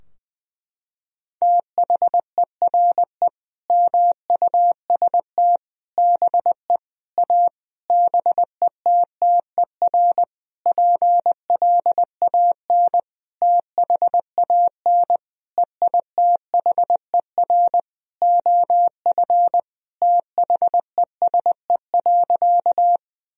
Morse code, heard as a keyed tone, about 20 wpm, 700 Hz. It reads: THERE MUST BE A BETTER PLAN THAN EITHER OF THESE.
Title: Sentences from Top 1000 Words - Review Part 4 - 20wpm